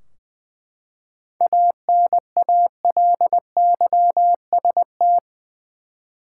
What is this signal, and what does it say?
Morse code, heard as a keyed tone, about 20 wpm, 700 Hz. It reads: ANALYST
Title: Practice QSO - Exchange 3 - More Info - 20wpm